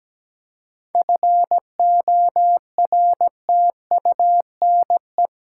Morse code, heard as a keyed tone, about 17 wpm, 700 Hz. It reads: FORTUNE
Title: Seven Letter Words - 17wpm